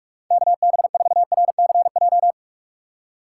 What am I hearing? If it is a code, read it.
Morse code, heard as a keyed tone, about 46 wpm, 700 Hz. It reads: KB4RCJ
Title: USA Call Signs - 4x Character Spacing - 45wpm